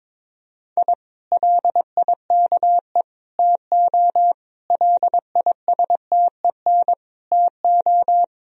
Morse code, heard as a keyed tone, about 21 wpm, 700 Hz. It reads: I LIKE TO LISTEN TO
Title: Sentences from Top 1000 Words - Review Part 3 - 22wpm